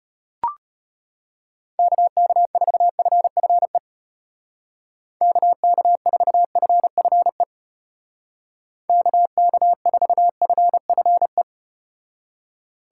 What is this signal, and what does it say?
Morse code, heard as a keyed tone, about 35 wpm, 700 Hz. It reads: KK4FFE KK4FFE KK4FFE